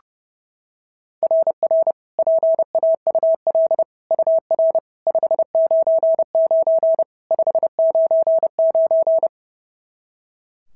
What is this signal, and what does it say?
Morse code, heard as a keyed tone, about 28 wpm, 660 Hz. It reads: RR PAUL UR 599 599
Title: SOTA QSO Practice 30wpm